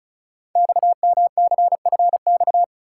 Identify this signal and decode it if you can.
Morse code, heard as a keyed tone, about 36 wpm, 690 Hz. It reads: XMCFX
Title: Code Groups - 5 Letters - 35wpm